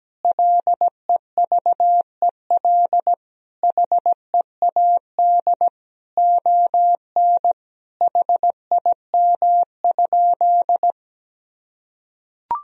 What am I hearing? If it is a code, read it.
Morse code, heard as a keyed tone, about 17 wpm, 700 Hz. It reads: LEVEL HEAD ON HIM?
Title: Sentences from Top 1000 Words - Review Part 4 - 17wpm